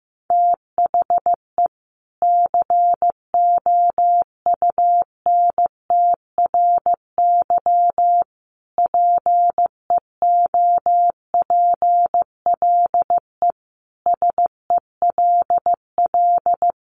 Morse code, heard as a keyed tone, about 15 wpm, 700 Hz. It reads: THE COUNTRY PEOPLE SELL
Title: Sentences from Top 1100 Words 15wpm